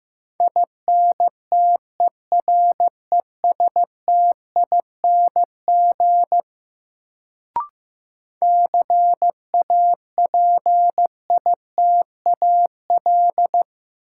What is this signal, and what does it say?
Morse code, heard as a keyed tone, about 15 wpm, 700 Hz. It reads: INTERESTING CAPITAL
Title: Sentences from Top 1000 Words - New Word Drill - 15wpm